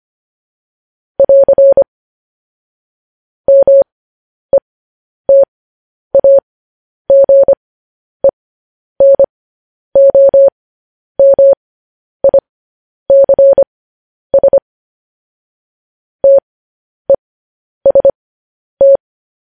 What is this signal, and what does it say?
Morse code, heard as a keyed tone, about 25 wpm, spaced as for 10 wpm, 550 Hz. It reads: <AR> METAGENOMICS TEST
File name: podcast / News headlines-1x-25x10